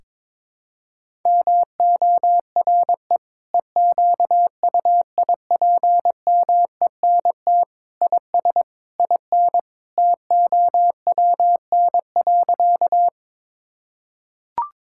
Morse code, heard as a keyed tone, about 21 wpm, 700 Hz. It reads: MORE EQUIPMENT IS IN TOWN.
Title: Sentences from Top 1000 Words - Review Part 5 - 22wpm